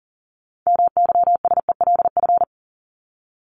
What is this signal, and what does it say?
Morse code, heard as a keyed tone, about 40 wpm, 700 Hz. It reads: MYSELF
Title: Top 1001-1250 Words 40wpm